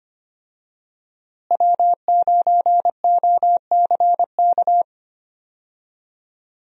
Morse code, heard as a keyed tone, about 25 wpm, 700 Hz. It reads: W9OCK
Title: USA Call Signs - 5 Characters - Repeated 2x - 25wpm